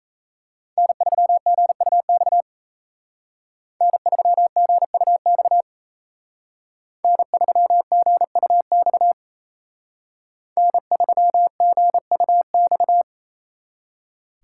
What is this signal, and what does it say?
Morse code, heard as a keyed tone, about 35 wpm, 700 Hz. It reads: N3GUX N3GUX N3GUX N3GUX